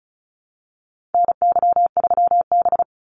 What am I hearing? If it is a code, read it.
Morse code, heard as a keyed tone, about 34 wpm, 700 Hz. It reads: NY3B
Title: Call Signs Repeated 3x - 35wpm